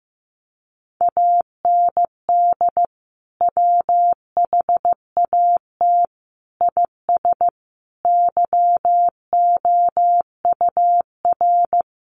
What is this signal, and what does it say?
Morse code, heard as a keyed tone, about 15 wpm, 700 Hz. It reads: AND WHAT IS YOUR
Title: Sentences from Top 1200 Words 15wpm